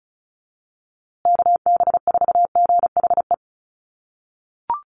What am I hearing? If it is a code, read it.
Morse code, heard as a keyed tone, about 34 wpm, 700 Hz. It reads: KB4GHE
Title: USA Call Signs Repeated 3x 35wpm